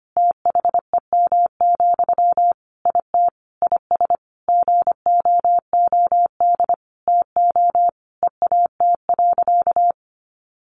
Morse code, heard as a keyed tone, about 25 wpm, 700 Hz. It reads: THEM, IT IS GOOD TO EAT.
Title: Sentences from Top 800 Words 25wpm